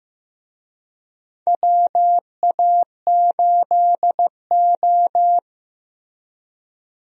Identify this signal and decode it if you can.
Morse code, heard as a keyed tone, about 15 wpm, 700 Hz. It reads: WA8O
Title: USA Call Signs - 4 Characters - Repeated 3x - 15wpm